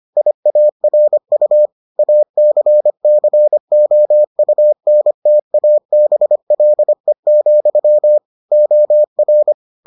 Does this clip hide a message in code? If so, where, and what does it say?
Yes: Morse code from 0.2 to 9.5 s, heard as a keyed tone, about 25 wpm, 600 Hz. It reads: IARU ACCOUNTABLE, OR